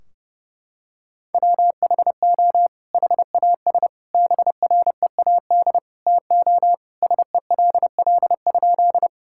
Morse code, heard as a keyed tone, about 32 wpm, 700 Hz. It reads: WHO HAS BREAD TO SELL?
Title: Sentences from Top 1600 Words 30wpm